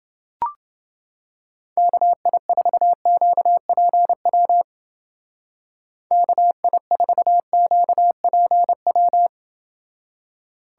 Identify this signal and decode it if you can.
Morse code, heard as a keyed tone, about 28 wpm, 700 Hz. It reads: KI4QPW KI4QPW